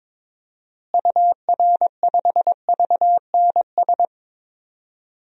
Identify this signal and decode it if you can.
Morse code, heard as a keyed tone, about 22 wpm, 700 Hz. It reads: UR5VNS
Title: DXCC-100 Call Signs - 6 Characters - Repeated 2x - 22wpm